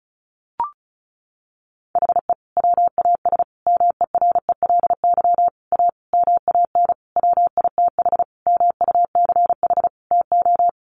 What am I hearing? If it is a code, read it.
Morse code, heard as a keyed tone, about 34 wpm, 700 Hz. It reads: HE WAS MERELY A MAN WITH MUCH TO
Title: Sentences from Top 1400 Words 35wpm